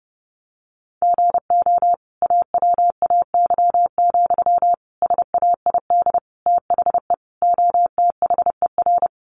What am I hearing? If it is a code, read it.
Morse code, heard as a keyed tone, about 30 wpm, 700 Hz. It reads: GO AWAY, SAID THE OTHER
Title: Sentences from Top 1500 Words 30wpm